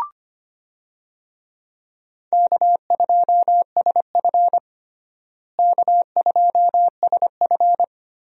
Morse code, heard as a keyed tone, about 25 wpm, 700 Hz. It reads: K2SF K2SF
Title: USA Call Signs - 4 Characters - Repeated 3x - 25wpm